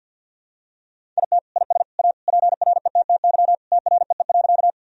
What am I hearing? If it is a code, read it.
Morse code, heard as a keyed tone, about 45 wpm, 700 Hz. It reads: IT IS A PRETTY TREE.
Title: Sentences from Top 1000 Words - Review Part 1 - 50wpm